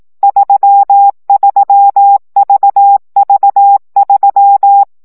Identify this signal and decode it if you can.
Morse code, heard as a keyed tone, about 18 wpm, 800 Hz. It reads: 33VV3
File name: CW gyakorlas v 3 15 baud